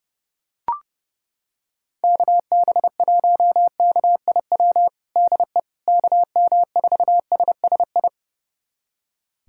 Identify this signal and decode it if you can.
Morse code, heard as a keyed tone, about 30 wpm, 700 Hz. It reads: KB1KIW DE KM4SSI
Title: USA Callsign DE Callsign 30wpm